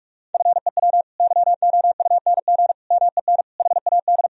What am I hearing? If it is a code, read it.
Morse code, heard as a keyed tone, about 46 wpm, 700 Hz. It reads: FEW YOUNG MEN HAD
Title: Sentences from Top 1000 Words - Review Part 1 - 45wpm